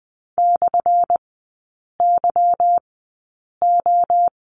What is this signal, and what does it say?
Morse code, heard as a keyed tone, about 20 wpm, spaced as for 8 wpm, 700 Hz. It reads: /YO
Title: 178 Lesson 29e - Slash - Three Characters 20wpm